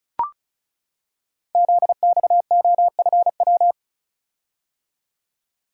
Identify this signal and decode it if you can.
Morse code, heard as a keyed tone, about 36 wpm, 700 Hz. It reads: ZXOFW
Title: Code Groups - 5 Letter:Numbers - 35wpm